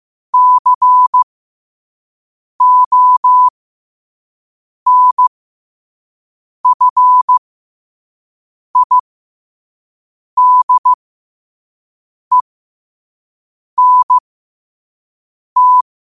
Morse code, heard as a keyed tone, about 15 wpm, spaced as for 5 wpm, 1000 Hz. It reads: CONFIDENT
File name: qotdfiles / 5wpm